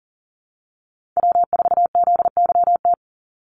Morse code, heard as a keyed tone, about 40 wpm, 700 Hz. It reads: W4ZYT
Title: Call Signs Repeated 3x - 40wpm